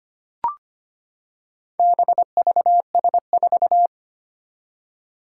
Morse code, heard as a keyed tone, about 25 wpm, 700 Hz. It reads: BVS4
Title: Four Challenging Letter-Number Combinations 25wpm